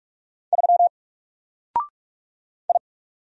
Morse code, heard as a keyed tone, about 44 wpm, 700 Hz. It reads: 3 I
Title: Single Character (Letter:Number:Punctuation:Prosign) 45wpm